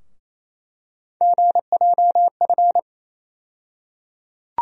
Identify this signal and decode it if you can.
Morse code, heard as a keyed tone, about 28 wpm, 700 Hz. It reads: GJF
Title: USA Call Sign Postfixes - 3 Letters 28wpm